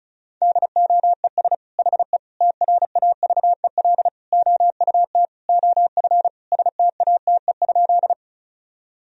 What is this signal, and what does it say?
Morse code, heard as a keyed tone, about 36 wpm, 700 Hz. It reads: DOES HE TRAVEL OUT OF STATE?
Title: Sentences from Top 1000 Words - Review Part 2 - 35wpm